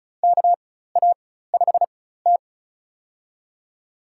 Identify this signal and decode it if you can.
Morse code, heard as a keyed tone, about 34 wpm, spaced as for 16 wpm, 700 Hz. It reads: KA5T
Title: USA Call Signs - 4 Characters - 4x Character Spacing - 35wpm